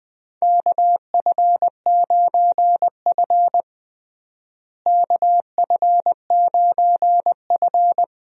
Morse code, heard as a keyed tone, about 20 wpm, 700 Hz. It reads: KF9F KF9F
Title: USA Call Signs - 4 Characters - Repeated 2x - 20wpm